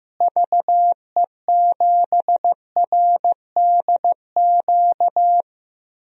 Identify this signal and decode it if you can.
Morse code, heard as a keyed tone, about 16 wpm, 700 Hz. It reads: VE7RDQ
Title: DXCC-100 Call Signs - 6 Characters - Repeated 3x - 15wpm